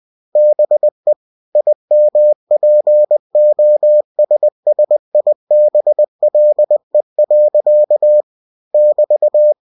Morse code, heard as a keyed tone, about 20 wpm, 600 Hz. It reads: BE IMPOSSIBLE. <BT>
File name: cw-books / winnie-the-pooh / winnie-20wpm-page-0063